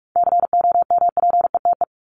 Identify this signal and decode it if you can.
Morse code, heard as a keyed tone, about 44 wpm, 700 Hz. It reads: COMPETE
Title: Top 1751-2000 Words 45wpm